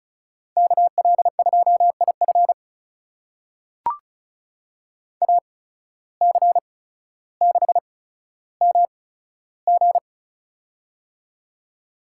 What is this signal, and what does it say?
Morse code, heard as a keyed tone, about 36 wpm, spaced as for 15 wpm, 700 Hz. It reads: KL2IF AC6MG
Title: USA Call Signs - 5 Characters - 8x Character Spacing - 35wpm